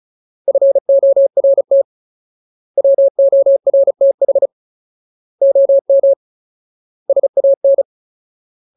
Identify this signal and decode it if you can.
Morse code, heard as a keyed tone, about 34 wpm, 550 Hz. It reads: FORT WORTH OM SAN
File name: Ultimate Word List / 35WPM Ultimate